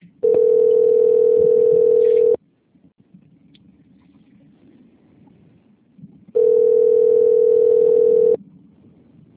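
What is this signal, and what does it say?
Ringback tone (North America), 440 + 480 Hz.
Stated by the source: Telephone call tone on an outgoing call recorded at 154.12.237.92 Centro Rio 2.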